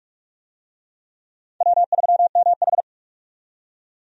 Morse code, heard as a keyed tone, about 41 wpm, 700 Hz. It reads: W3MH
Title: USA Call Signs - 4 Characters - 45wpm